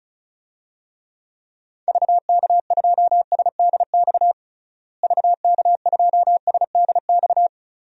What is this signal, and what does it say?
Morse code, heard as a keyed tone, about 36 wpm, 700 Hz. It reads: VK2SDX VK2SDX
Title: DXCC-100 Call Signs - 6 Characters - Repeated 3x - 35wpm